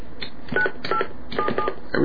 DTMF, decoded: 9877